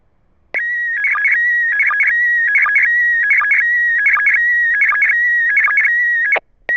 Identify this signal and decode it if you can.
SSTV (slow-scan television), PD 180.
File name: iss-sstv